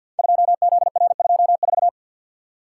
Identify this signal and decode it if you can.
Morse code, heard as a keyed tone, about 52 wpm, 700 Hz. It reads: 2ZR24
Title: Code Groups - 5 Letter:Numbers - 50wpm